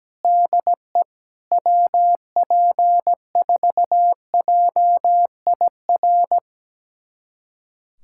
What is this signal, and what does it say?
Morse code, heard as a keyed tone, about 17 wpm, 700 Hz. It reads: DE WP4JIR
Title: USA Callsign DE Callsign 17wpm